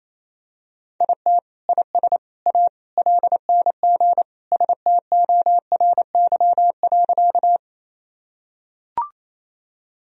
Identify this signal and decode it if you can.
Morse code, heard as a keyed tone, about 28 wpm, 700 Hz. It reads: IT IS A LNG STORY.
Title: Sentences from Top 300 Words and Abbreviations 28wpm